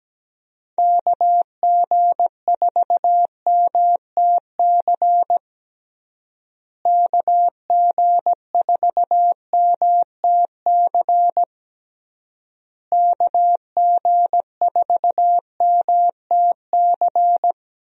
Morse code, heard as a keyed tone, about 17 wpm, 700 Hz. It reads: KG4MTC KG4MTC KG4MTC
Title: USA Call Signs - 6 Characters - Repeated 3x - 17wpm